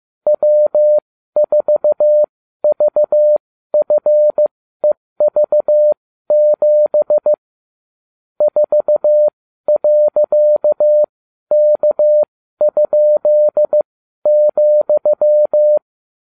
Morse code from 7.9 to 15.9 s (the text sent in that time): 4.K?,